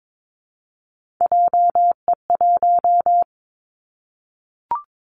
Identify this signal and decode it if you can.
Morse code, heard as a keyed tone, about 22 wpm, 700 Hz. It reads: JE1
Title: Callsign Prefixes for Top 25 DXCC - Repeated 3x - 22wpm